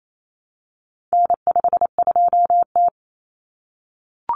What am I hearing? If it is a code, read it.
Morse code, heard as a keyed tone, about 28 wpm, 700 Hz. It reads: N52T
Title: Common Cut Numbers - 4-Digit 28wpm